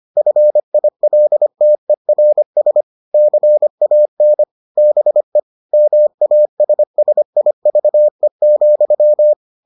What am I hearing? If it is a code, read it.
Morse code, heard as a keyed tone, about 25 wpm, 600 Hz. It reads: FILTERS CAN BE MASSIVE,